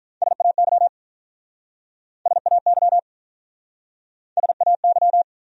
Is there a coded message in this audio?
Yes: Morse code from 0.2 to 5.2 s, heard as a keyed tone, about 47 wpm, 700 Hz. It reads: SAY SAY SAY